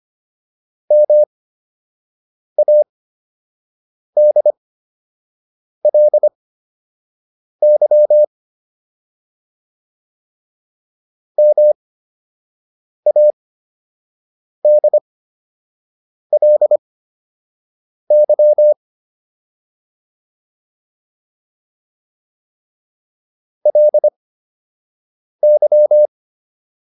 Morse code, heard as a keyed tone, about 25 wpm, spaced as for 6 wpm, 600 Hz. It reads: MADLY MADLY LY